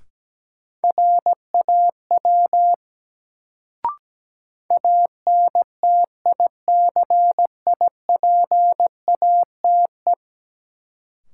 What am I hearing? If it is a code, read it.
Morse code, heard as a keyed tone, about 17 wpm, 700 Hz. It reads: RAW ANTICIPATE
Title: Sentences from Top 1500 Words - New Word Drill - 17wpm